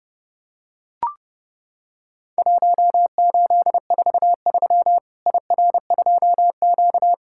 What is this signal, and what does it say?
Morse code, heard as a keyed tone, about 32 wpm, 700 Hz. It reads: 1843 IR2Q
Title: Worked All Europe DX Contest - QTC Exchange - 30wpm